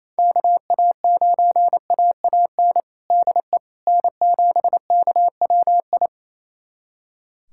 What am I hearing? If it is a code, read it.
Morse code, heard as a keyed tone, about 28 wpm, 700 Hz. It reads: KA9AAN DE N7KWI
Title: USA Callsign DE Callsign 28wpm